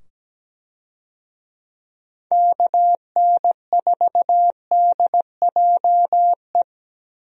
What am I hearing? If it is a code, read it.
Morse code, heard as a keyed tone, about 17 wpm, 700 Hz. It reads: KN4DJE
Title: USA Call Signs - 6 Characters - 17wpm